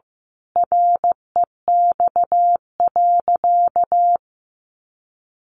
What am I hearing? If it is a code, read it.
Morse code, heard as a keyed tone, about 15 wpm, 700 Hz. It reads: REX.
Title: A Sound Of Thunder 15wpm